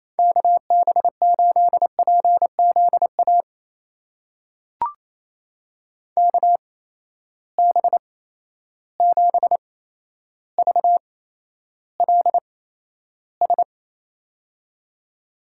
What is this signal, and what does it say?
Morse code, heard as a keyed tone, about 28 wpm, spaced as for 12 wpm, 700 Hz. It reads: KB8PZA KB7VLS
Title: USA Call Signs - 6 Characters - 8x Character Spacing - 28wpm